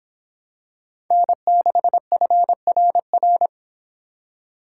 Morse code, heard as a keyed tone, about 24 wpm, 700 Hz. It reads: N6FRR